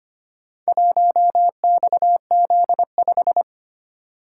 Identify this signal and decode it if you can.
Morse code, heard as a keyed tone, about 25 wpm, 700 Hz. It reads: 1XZ5
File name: practice / 25wpm / Four Letter-Number Combinations - 25wpm